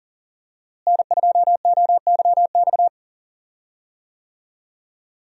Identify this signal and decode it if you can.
Morse code, heard as a keyed tone, about 40 wpm, 700 Hz. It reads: N2OYX
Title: USA Call Signs - 5 Characters - Repeated 2x - 40wpm